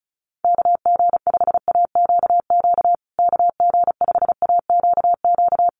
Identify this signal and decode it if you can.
Morse code, heard as a keyed tone, about 34 wpm, 700 Hz. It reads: KG5AQQ KG5AQQ
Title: USA Call Signs Repeated 3x 35wpm